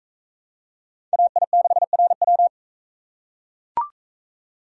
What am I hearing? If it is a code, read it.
Morse code, heard as a keyed tone, about 41 wpm, 700 Hz. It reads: AI6RW